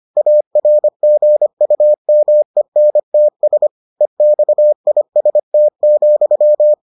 Morse code, heard as a keyed tone, about 25 wpm, 600 Hz. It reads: ARGUMENTS EXIST,